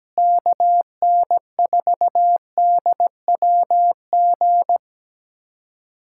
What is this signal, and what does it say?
Morse code, heard as a keyed tone, about 17 wpm, 700 Hz. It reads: KN4DWG
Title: USA Callsign DE Callsign 17wpm